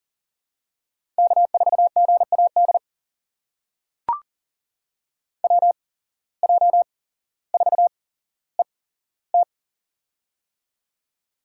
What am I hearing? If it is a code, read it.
Morse code, heard as a keyed tone, about 40 wpm, spaced as for 17 wpm, 700 Hz. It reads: K4GAD WJ4ET
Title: USA Call Signs - 5 Characters - 8x Character Spacing - 40wpm